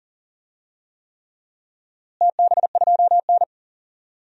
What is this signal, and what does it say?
Morse code, heard as a keyed tone, about 40 wpm, 700 Hz. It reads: TB2N
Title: DXCC-100 Call Signs - 4 Characters - Repeated 3x - 40wpm